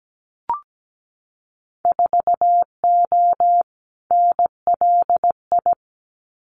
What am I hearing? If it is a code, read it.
Morse code, heard as a keyed tone, about 17 wpm, 700 Hz. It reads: 4O NLI
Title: Winter Field Day Exchanges Encore 17wpm